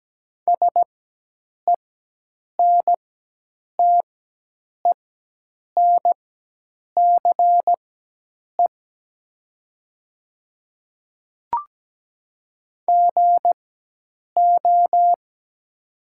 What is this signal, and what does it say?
Morse code, heard as a keyed tone, about 17 wpm, spaced as for 8 wpm, 700 Hz. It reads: SENTENCE GO
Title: 4x Character Spacing - Top 200 Words - 17wpm